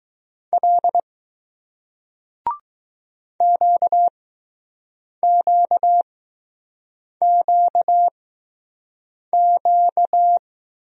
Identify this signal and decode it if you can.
Morse code, heard as a keyed tone, about 20 wpm, spaced as for 6 wpm, 700 Hz. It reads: L QQQQ